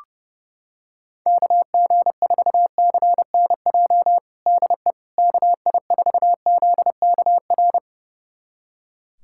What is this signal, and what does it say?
Morse code, heard as a keyed tone, about 30 wpm, 700 Hz. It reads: KG4CNJ DE KI4ZKR
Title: USA Callsign DE Callsign - 6 Characters - 30wpm